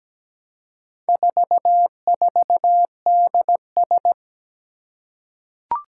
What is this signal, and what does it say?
Morse code, heard as a keyed tone, about 17 wpm, 700 Hz. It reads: 44DS
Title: Four Challenging Letter-Number Combinations 17wpm